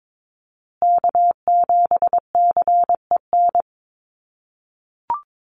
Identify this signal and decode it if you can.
Morse code, heard as a keyed tone, about 22 wpm, 700 Hz. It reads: K7CEN
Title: USA Call Signs Repeated 3x 22wpm